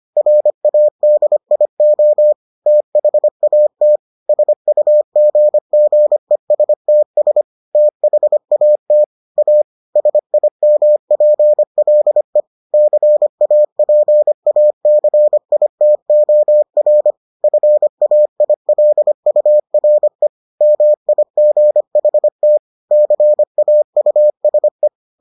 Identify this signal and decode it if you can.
Morse code, heard as a keyed tone, about 25 wpm, 600 Hz. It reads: RADIO THAT SUGGESTS THAT A SIMPLE CAPACITOR FAILURE MIGHT CAUSE